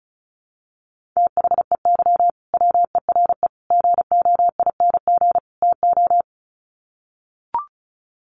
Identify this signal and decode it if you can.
Morse code, heard as a keyed tone, about 34 wpm, 700 Hz. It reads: THEY WERE GOING TO
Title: Sets of 4 Words 35wpm